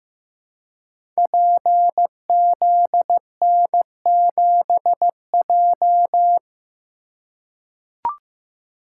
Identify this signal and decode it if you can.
Morse code, heard as a keyed tone, about 15 wpm, 700 Hz. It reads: PZN7J
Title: Code Groups - 5 Letter:Numbers - 15wpm